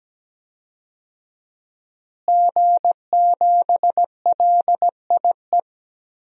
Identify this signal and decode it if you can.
Morse code, heard as a keyed tone, about 17 wpm, 700 Hz. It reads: G7LIE
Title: DXCC-100 Call Signs - 5 Characters - Repeated 2x - 17wpm